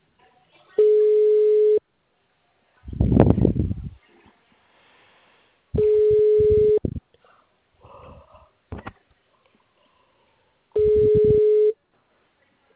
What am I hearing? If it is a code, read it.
Ringback tone (Europe, 425 Hz).
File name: out- -2009- - - .68686